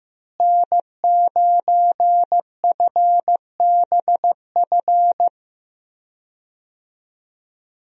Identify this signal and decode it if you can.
Morse code, heard as a keyed tone, about 15 wpm, 700 Hz. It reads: N9FBF
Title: USA Call Signs - 5 Characters - Repeated 2x - 15wpm